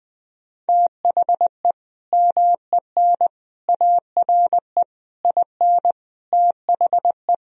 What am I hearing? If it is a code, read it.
Morse code, heard as a keyed tone, about 20 wpm, 700 Hz. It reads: THE MEN ARE IN THE
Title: Sentences from Top 1000 Words - Review Part 1 - 20wpm